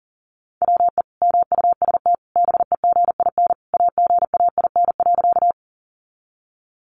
Morse code, heard as a keyed tone, about 40 wpm, 700 Hz. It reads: WE MUST BEGIN AGAIN.
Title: Sentences from Top 300 Words 40wpm